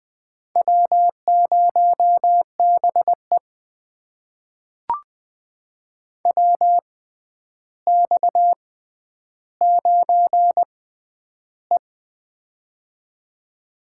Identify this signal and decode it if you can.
Morse code, heard as a keyed tone, about 20 wpm, spaced as for 10 wpm, 700 Hz. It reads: W0BE WX9E
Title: USA Call Signs - 4 Characters - 6x Character Spacing - 20wpm